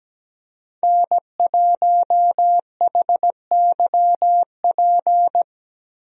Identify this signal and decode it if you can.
Morse code, heard as a keyed tone, about 17 wpm, 700 Hz. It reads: N1HYP
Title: USA Call Signs - 5 Characters - Repeated 2x - 17wpm